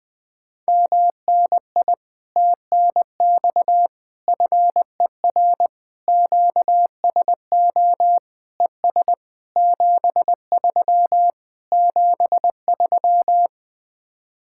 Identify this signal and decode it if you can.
Morse code, heard as a keyed tone, about 20 wpm, 700 Hz. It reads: MNI TNX FER QSO ES 73 73
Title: Speed-Racing Common QSO Phrases Encore - 17wpm